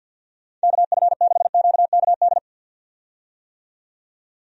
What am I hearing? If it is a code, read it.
Morse code, heard as a keyed tone, about 53 wpm, 700 Hz. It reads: KF6QKD
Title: USA Call Signs - 6 Characters - 50wpm